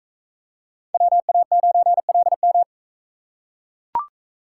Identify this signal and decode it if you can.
Morse code, heard as a keyed tone, about 39 wpm, 700 Hz. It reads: WA9LM